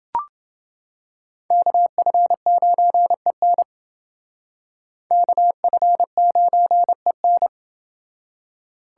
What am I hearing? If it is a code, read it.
Morse code, heard as a keyed tone, about 29 wpm, 700 Hz. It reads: KF9EN KF9EN